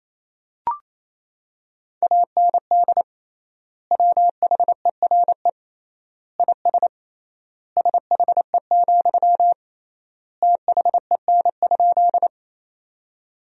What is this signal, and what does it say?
Morse code, heard as a keyed tone, about 27 wpm, 700 Hz. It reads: AND WHERE IS SHE, THEN?
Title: 3x Word-Spacing - Sentences from Top 200 Words - 28wpm